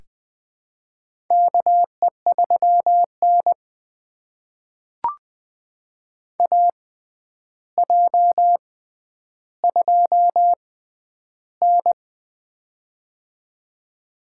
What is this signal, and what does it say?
Morse code, heard as a keyed tone, about 20 wpm, spaced as for 10 wpm, 700 Hz. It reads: KE3N AJ2N
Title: USA Call Signs - 4 Characters - 6x Character Spacing - 20wpm